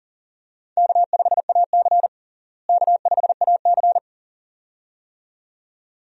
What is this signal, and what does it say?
Morse code, heard as a keyed tone, about 40 wpm, 700 Hz. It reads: K5AC K5AC
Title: USA Call Signs - 4 Characters - Repeated 2x - 40wpm